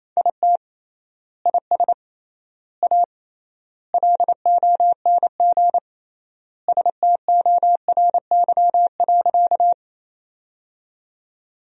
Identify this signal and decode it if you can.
Morse code, heard as a keyed tone, about 28 wpm, 700 Hz. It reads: IT IS A LONG STORY.
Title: 3x Word-Spacing - Sentences from Top 300 Words - 28wpm